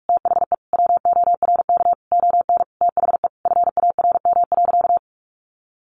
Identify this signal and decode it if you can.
Morse code, heard as a keyed tone, about 44 wpm, 700 Hz. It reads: THE WORK ON THE FARM.
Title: Sentences from Top 1200 Words 45wpm